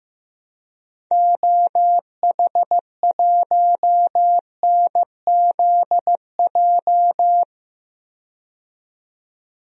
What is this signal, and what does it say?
Morse code, heard as a keyed tone, about 15 wpm, 700 Hz. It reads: OH1NZJ
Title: DXCC-100 Call Signs - 6 Characters - 15wpm